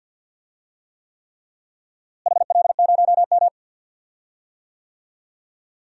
Morse code, heard as a keyed tone, about 52 wpm, 700 Hz. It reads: HL0M
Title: DXCC-100 Call Signs - 4 Characters - Repeated 3x - 50wpm